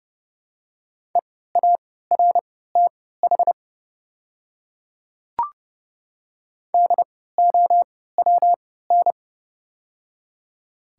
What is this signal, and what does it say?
Morse code, heard as a keyed tone, about 31 wpm, spaced as for 17 wpm, 700 Hz. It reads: EARTH DOWN
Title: 3x Character Spacing - Top 200 Words - 30wpm